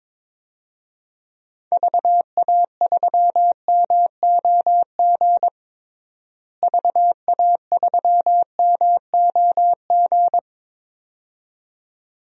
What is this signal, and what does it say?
Morse code, heard as a keyed tone, about 22 wpm, 700 Hz. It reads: VA3MOG VA3MOG
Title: DXCC-100 Call Signs - 6 Characters - Repeated 2x - 22wpm